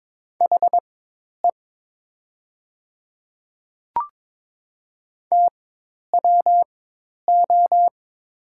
Morse code, heard as a keyed tone, about 22 wpm, spaced as for 10 wpm, 700 Hz. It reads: HE TWO
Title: 4x Character Spacing - Top 300 Words - 22wpm